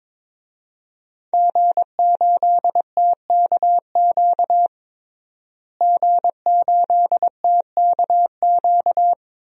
Morse code, heard as a keyed tone, about 21 wpm, 700 Hz. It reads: G8TKQ G8TKQ
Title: DXCC-100 Call Signs - 5 Characters - Repeated 2x - 22wpm